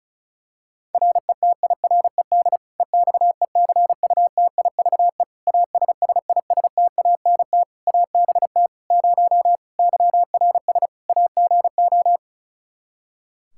Morse code, heard as a keyed tone, about 34 wpm, 700 Hz. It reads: RETIRED EXECUTIVE ASSISTANT ABT 0 YRS AGO
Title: K1USN SST QSO Practice 35wpm